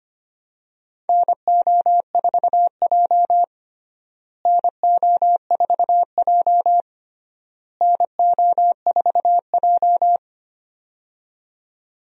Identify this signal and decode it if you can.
Morse code, heard as a keyed tone, about 25 wpm, 700 Hz. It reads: NO4J NO4J NO4J
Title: USA Call Signs - 4 Characters - Repeated 3x - 25wpm